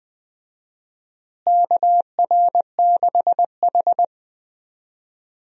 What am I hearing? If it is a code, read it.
Morse code, heard as a keyed tone, about 20 wpm, 700 Hz. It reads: KR6H
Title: USA Call Signs - 4 Characters - Repeated 3x - 20wpm